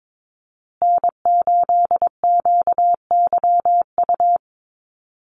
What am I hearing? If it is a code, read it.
Morse code, heard as a keyed tone, about 22 wpm, 700 Hz. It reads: N8QYU
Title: USA Call Signs Repeated 2x 22wpm